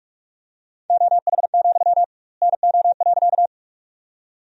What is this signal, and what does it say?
Morse code, heard as a keyed tone, about 41 wpm, 700 Hz. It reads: OH, NO.
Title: 2x Word-Spacing - Sentences from Top 500 Words - 45wpm